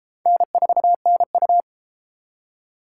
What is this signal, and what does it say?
Morse code, heard as a keyed tone, about 30 wpm, 700 Hz. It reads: N4NU